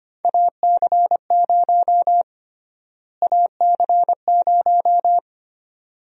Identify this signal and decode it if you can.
Morse code, heard as a keyed tone, about 25 wpm, 700 Hz. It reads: AC0 AC0
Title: USA Call Sign Prefixes - Repeated 2x - 25wpm